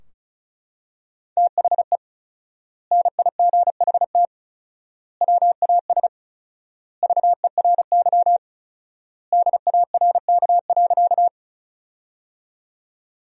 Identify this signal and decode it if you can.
Morse code, heard as a keyed tone, about 34 wpm, 700 Hz. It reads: THE NIGHT WAS VERY DARK.
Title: 4x Word-Spacing - Sentences from Top 500 Words - 35wpm